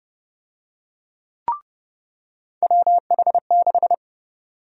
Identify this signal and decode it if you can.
Morse code, heard as a keyed tone, about 31 wpm, 700 Hz. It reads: WH6
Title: USA Call Sign Prefixes - 30wpm